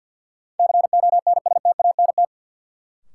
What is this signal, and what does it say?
Morse code, heard as a keyed tone, about 52 wpm, 700 Hz. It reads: CONSTANT
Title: Sentences from Top 1500 Words - New Word Drill - 50wpm